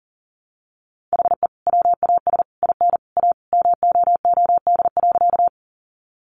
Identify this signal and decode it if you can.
Morse code, heard as a keyed tone, about 40 wpm, 700 Hz. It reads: HE WAS IN A MOOD.
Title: Sentences from Top 1100 Words 40wpm